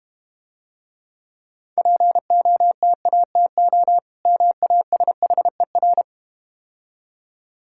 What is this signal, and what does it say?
Morse code, heard as a keyed tone, about 32 wpm, 700 Hz. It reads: POTATO MASHER